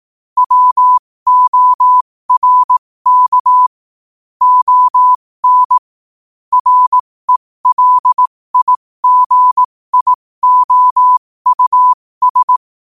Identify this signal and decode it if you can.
Morse code, heard as a keyed tone, about 18 wpm, spaced as for 15 wpm, 1000 Hz. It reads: WORK ON RELIGIOUS
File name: qotdfiles / 15wpm